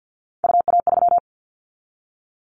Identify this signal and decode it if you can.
Morse code, heard as a keyed tone, about 49 wpm, 700 Hz. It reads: UA3
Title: Callsign Prefixes for Top 100 DXCC - 50wpm